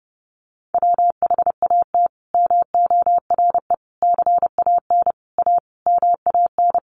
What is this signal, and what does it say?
Morse code, heard as a keyed tone, about 30 wpm, 700 Hz. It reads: WHAT MORE CAN A MAN
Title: Sentences from Top 1100 Words 30wpm